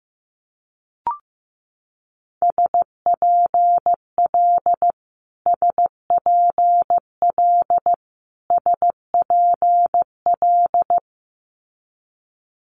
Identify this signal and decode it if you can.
Morse code, heard as a keyed tone, about 15 wpm, 700 Hz. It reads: SPL SPL SPL
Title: Multi-letter Phonemes Repeated 3x 15wpm